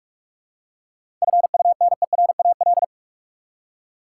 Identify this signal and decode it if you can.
Morse code, heard as a keyed tone, about 46 wpm, 700 Hz. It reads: FUNERAL
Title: Seven Letter Words - 45wpm